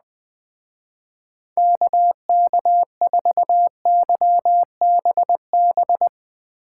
Morse code, heard as a keyed tone, about 20 wpm, 700 Hz. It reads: KK4YBB